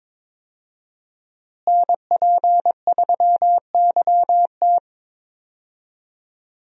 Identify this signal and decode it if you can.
Morse code, heard as a keyed tone, about 22 wpm, 700 Hz. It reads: NP3YT
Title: USA Call Signs - 5 Characters - Repeated 2x - 22wpm